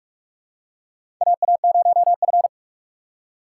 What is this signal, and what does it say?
Morse code, heard as a keyed tone, about 46 wpm, 700 Hz. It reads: AA0F
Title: USA Call Signs - 4 Characters - 4x Character Spacing - 45wpm